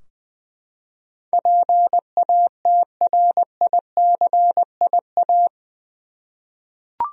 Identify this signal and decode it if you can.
Morse code, heard as a keyed tone, about 20 wpm, 700 Hz. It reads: PATRICIA
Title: USA Common Names - 20wpm